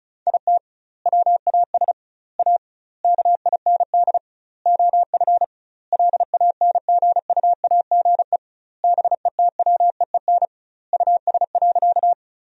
Morse code, heard as a keyed tone, about 34 wpm, 700 Hz. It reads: IT WAS A KIND OF LANGUAGE BETWEEN US.
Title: 2x Word-Spacing - Sentences from Top 500 Words - 35wpm